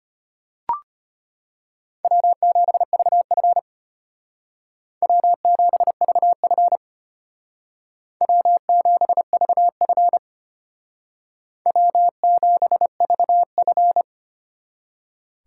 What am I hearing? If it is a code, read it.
Morse code, heard as a keyed tone, about 32 wpm, 700 Hz. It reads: W7VF W7VF W7VF W7VF